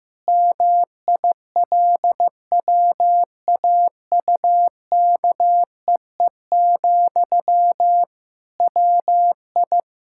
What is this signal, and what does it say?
Morse code, heard as a keyed tone, about 15 wpm, 700 Hz. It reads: MILWAUKEE, WI
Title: Practice QSO - Exchange 1 - Beginning - 15wpm